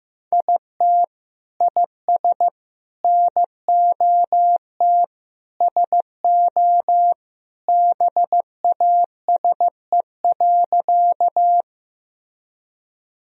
Morse code, heard as a keyed tone, about 15 wpm, 700 Hz. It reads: IT IS NOT SO BASE.
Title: Sentences from Top 1000 Words - Review Part 1 - 15wpm